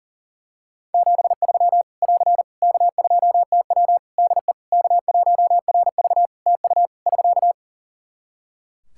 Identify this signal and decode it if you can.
Morse code, heard as a keyed tone, about 40 wpm, 700 Hz. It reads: 73 <AR> K2TW DE K1RV TU <SK>
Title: K1USN SST QSO Practice 40wpm